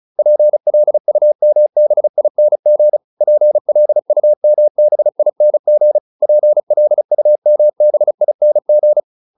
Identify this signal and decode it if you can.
Morse code, heard as a keyed tone, about 34 wpm, 600 Hz. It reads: PLUMBING PLUMBING PLUMBING